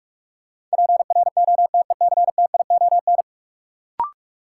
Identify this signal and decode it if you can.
Morse code, heard as a keyed tone, about 46 wpm, 700 Hz. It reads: PROTECTION
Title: Sentences from Top 1600 Words - New Word Drill Alt - 45wpm